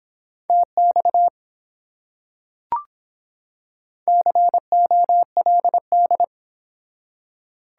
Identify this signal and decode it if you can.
Morse code, heard as a keyed tone, about 25 wpm, 700 Hz. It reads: TX COLD